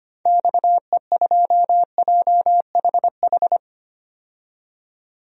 Morse code, heard as a keyed tone, about 25 wpm, 700 Hz. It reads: XE2JHH
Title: DXCC-100 Call Signs - 6 Characters - 25wpm